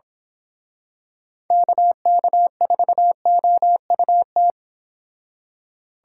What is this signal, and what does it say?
Morse code, heard as a keyed tone, about 26 wpm, 700 Hz. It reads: KK4OUT
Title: Speed Racing - USA 2x3 Call Signs 22wpm